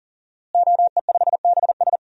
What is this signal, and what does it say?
Morse code, heard as a keyed tone, about 40 wpm, 700 Hz. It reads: OE5BS
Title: DXCC-100 Call Signs - 5 Characters - Repeated 3x - 40wpm